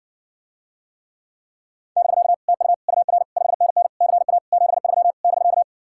Morse code, heard as a keyed tone, about 60 wpm, 700 Hz. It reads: <BK> TU UR 5NN CA 73 <BK>